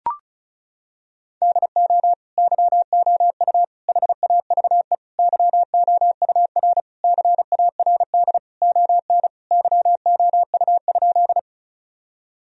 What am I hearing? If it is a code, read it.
Morse code, heard as a keyed tone, about 34 wpm, 700 Hz. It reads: DO YOU HAVE YOUR CARD ON YOU?
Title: Sentences from Top 1000 Words - Review Part 4 - 35wpm